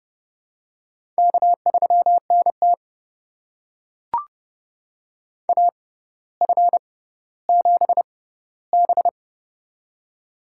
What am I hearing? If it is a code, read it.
Morse code, heard as a keyed tone, about 30 wpm, spaced as for 15 wpm, 700 Hz. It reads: K3NT AF7B
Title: USA Call Signs - 4 Characters - 6x Character Spacing - 30wpm